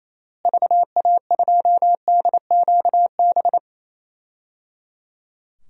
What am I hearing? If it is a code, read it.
Morse code, heard as a keyed tone, about 28 wpm, 700 Hz. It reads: VA2DQB
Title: DXCC-100 Call Signs - 6 Characters - Repeated 3x - 28wpm